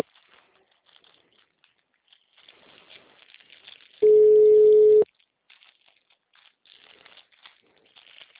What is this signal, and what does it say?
Ringback tone (Europe, 425 Hz).
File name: out- -2016- - - .82